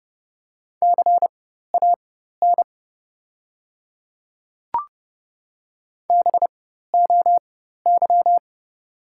Morse code, heard as a keyed tone, about 31 wpm, spaced as for 14 wpm, 690 Hz. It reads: CAN BOY
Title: 4x Character Spacing - Top 200 Words - 30wpm